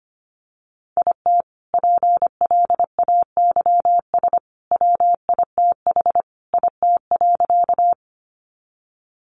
Morse code, heard as a keyed tone, about 25 wpm, 700 Hz. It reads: IT PLAYS WITH IT.
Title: Additional Easy Phrases 25wpm